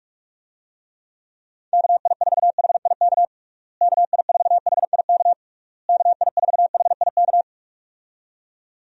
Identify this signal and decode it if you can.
Morse code, heard as a keyed tone, about 46 wpm, 700 Hz. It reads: KI4HIK KI4HIK KI4HIK
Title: USA Call Signs - 6 Characters - Repeated 3x - 45wpm